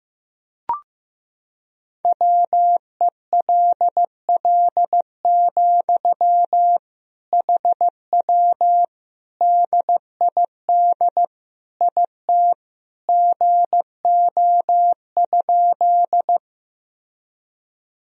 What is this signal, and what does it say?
Morse code, heard as a keyed tone, about 15 wpm, 700 Hz. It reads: WELL, HW DID IT GO?
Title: Sentences from Top 200 Words and Abbreviations 15wpm